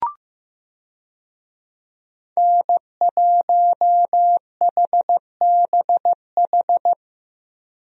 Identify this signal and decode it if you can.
Morse code, heard as a keyed tone, about 15 wpm, 700 Hz. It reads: N1HBH
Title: USA Call Signs - 5 Characters - Repeated 2x - 15wpm